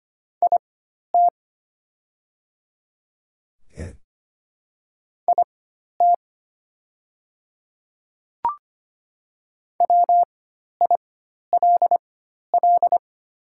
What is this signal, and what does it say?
Morse code, heard as a keyed tone, about 25 wpm, spaced as for 12 wpm, 700 Hz. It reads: IT IT WILL
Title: 4x Character Spacing - Top 100 Words - 25wpm